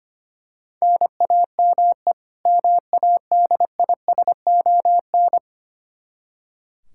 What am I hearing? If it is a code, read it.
Morse code, heard as a keyed tone, about 25 wpm, 700 Hz. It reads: NAME MADISON
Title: Practice QSO - Exchange 1 - Beginning - 25wpm